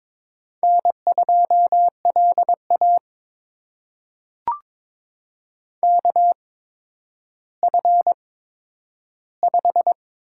Morse code, heard as a keyed tone, about 21 wpm, spaced as for 11 wpm, 700 Hz. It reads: N2LA KF5
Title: USA Call Signs - 8x Character Spacing - 22wpm